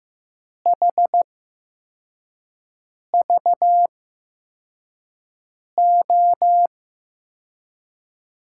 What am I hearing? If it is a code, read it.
Morse code, heard as a keyed tone, about 15 wpm, spaced as for 4 wpm, 700 Hz. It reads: HVO
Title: USA Call Signs - 8x Character Spacing - 15wpm